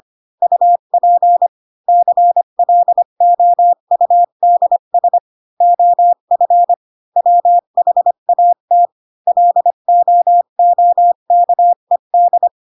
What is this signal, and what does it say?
Morse code, heard as a keyed tone, about 25 wpm, 700 Hz. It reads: UP CLOUDS OF WHAT LOOKED
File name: Pinocchio / Pinocchio ch3 20 25 ik1hge